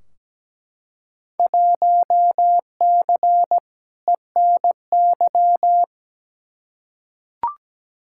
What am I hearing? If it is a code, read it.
Morse code, heard as a keyed tone, about 17 wpm, 700 Hz. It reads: 1C ENY
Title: Field Day Exchanges Encore - 17wpm